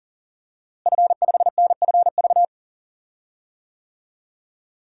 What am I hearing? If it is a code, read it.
Morse code, heard as a keyed tone, about 37 wpm, 700 Hz. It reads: F5NFV
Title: DXCC-100 Call Signs - 5 Characters - Repeated 3x - 40wpm